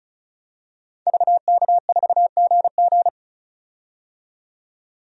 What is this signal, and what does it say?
Morse code, heard as a keyed tone, about 36 wpm, 700 Hz. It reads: VK4GG
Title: DXCC-100 Call Signs - 5 Characters - 35wpm